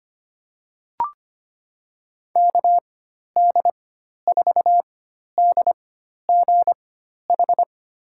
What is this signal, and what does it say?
Morse code, heard as a keyed tone, about 25 wpm, spaced as for 12 wpm, 700 Hz. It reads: KD4DGH
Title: USA Call Signs - 6 Characters - 4x Character Spacing - 25wpm